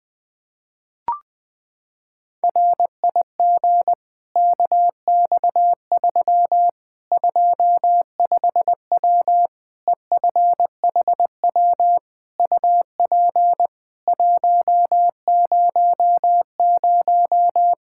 Morse code, heard as a keyed tone, about 20 wpm, 700 Hz. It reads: RIG KX3 25W EFHW UP 100
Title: Practice QSO - Exchange 2 - Info - 20wpm